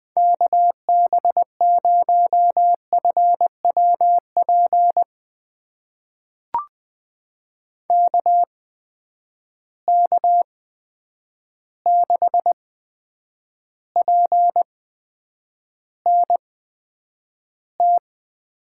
Morse code, heard as a keyed tone, about 20 wpm, spaced as for 9 wpm, 700 Hz. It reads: KB0FWP KK6PNT
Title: USA Call Signs - 6 Characters - 8x Character Spacing - 20wpm